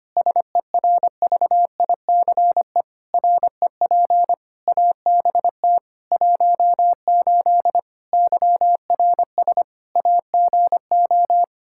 Morse code, heard as a keyed tone, about 25 wpm, 700 Hz. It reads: SERVICE REP ABT 18 YRS AGO
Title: Practice QSO - Exchange 3 - More Info - 25wpm